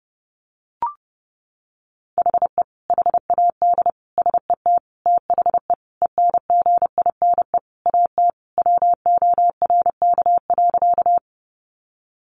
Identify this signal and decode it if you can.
Morse code, heard as a keyed tone, about 28 wpm, 700 Hz. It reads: HE HAD SET THE ENGINE AT WORK.
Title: Sentences from Top 1500 Words 30wpm